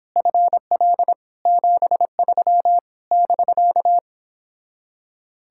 Morse code, heard as a keyed tone, about 31 wpm, 700 Hz. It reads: FL 73 <BK>